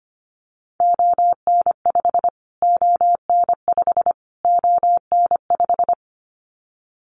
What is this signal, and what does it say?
Morse code, heard as a keyed tone, about 25 wpm, 700 Hz. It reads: ON5 ON5 ON5
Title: Callsign Prefixes for Top 25 DXCC - Repeated 3x - 25wpm